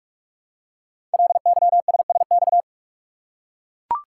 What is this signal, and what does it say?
Morse code, heard as a keyed tone, about 44 wpm, 700 Hz. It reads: LYSSX
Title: Code Groups - 5 Letters - 45wpm